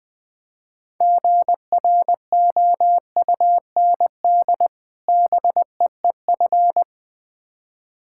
Morse code, heard as a keyed tone, about 20 wpm, 700 Hz. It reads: GROUND BEEF
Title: Difficult Words with Hints - 20wpm